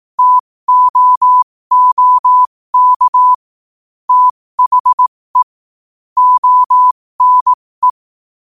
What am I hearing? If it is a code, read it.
Morse code, heard as a keyed tone, about 18 wpm, spaced as for 15 wpm, 1000 Hz. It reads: TOOK THE ONE
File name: qotdfiles / 15wpm